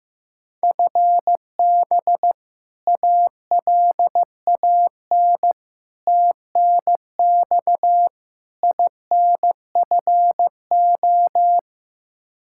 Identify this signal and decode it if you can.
Morse code, heard as a keyed tone, about 15 wpm, 700 Hz. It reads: FB ALAN TNX INFO
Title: Practice QSO - Exchange 3 - More Info - 15wpm